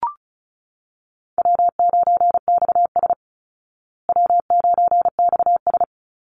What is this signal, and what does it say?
Morse code, heard as a keyed tone, about 34 wpm, 700 Hz. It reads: W9XS W9XS
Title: Call Signs Repeated 3x - 35wpm